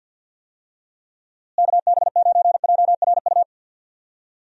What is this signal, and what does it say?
Morse code, heard as a keyed tone, about 51 wpm, 700 Hz. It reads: KB9JRU
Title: USA Call Signs - 6 Characters - Repeated 2x - 50wpm